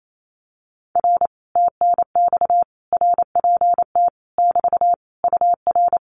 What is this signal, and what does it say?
Morse code, heard as a keyed tone, about 28 wpm, 700 Hz. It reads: R TNX RPT <BT> UR
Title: Common QSO Phrases Encore 28wpm